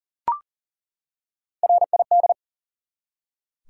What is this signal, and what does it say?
Morse code, heard as a keyed tone, about 39 wpm, 700 Hz. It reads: RID
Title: Three Letter Words - 40wpm